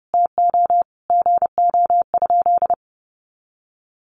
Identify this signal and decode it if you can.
Morse code, heard as a keyed tone, about 30 wpm, 700 Hz. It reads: TO GO?
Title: Sentences from Top 1300 Words 30wpm